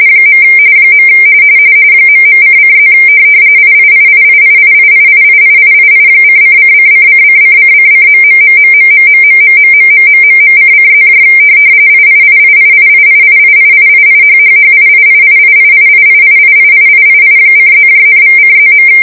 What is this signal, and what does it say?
RTTY, about 45.45 baud. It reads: . ..::+ IIIIIIIIIIIIIII:::::::::::...............::+ IIIIIIIIIIIIIIHHHIIIIIIIIIIIIIIII:+,=:+.